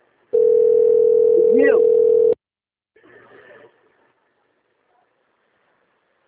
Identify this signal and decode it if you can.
Ringback tone (North America), 440 + 480 Hz, with speech.
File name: out- - - - - .28677